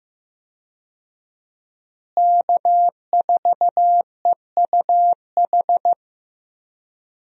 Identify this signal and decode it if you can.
Morse code, heard as a keyed tone, about 14 wpm, 700 Hz. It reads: K4EUH
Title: USA Call Signs - 5 Characters - Repeated 3x - 15wpm